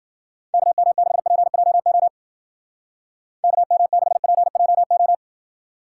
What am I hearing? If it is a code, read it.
Morse code, heard as a keyed tone, about 52 wpm, 700 Hz. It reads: KM6PJO KM6PJO